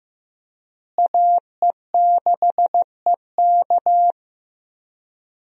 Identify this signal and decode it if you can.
Morse code, heard as a keyed tone, about 15 wpm, 700 Hz. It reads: AE6EK
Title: USA Call Signs - 5 Characters - Repeated 3x - 15wpm